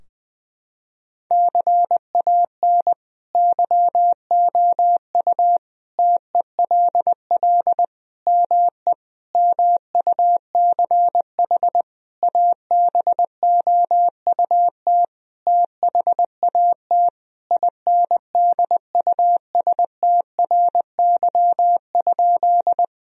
Morse code, heard as a keyed tone, about 20 wpm, 700 Hz. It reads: CAN YOU TELL ME MUCH ABOUT THAT INDUSTRY?
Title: Sentences from Top 1000 Words - Review Part 4 - 20wpm